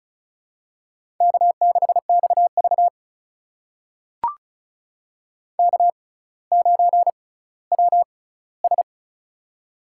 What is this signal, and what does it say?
Morse code, heard as a keyed tone, about 35 wpm, spaced as for 18 wpm, 700 Hz. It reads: K6XV K9WS
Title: USA Call Signs - 4 Characters - 6x Character Spacing - 35wpm